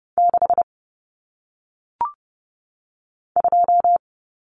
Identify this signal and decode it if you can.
Morse code, heard as a keyed tone, about 30 wpm, 700 Hz. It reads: 6 2
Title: Numbers - 1-Digit 30wpm